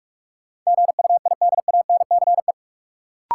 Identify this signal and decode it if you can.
Morse code, heard as a keyed tone, about 46 wpm, 700 Hz. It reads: GUIDANCE
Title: Sentences from Top 1600 Words - New Word Drill Alt - 45wpm